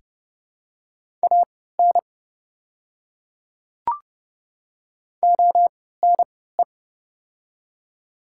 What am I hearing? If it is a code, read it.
Morse code, heard as a keyed tone, about 30 wpm, spaced as for 17 wpm, 700 Hz. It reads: AN ONE
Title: 3x Character Spacing - Top 100 Words - 30wpm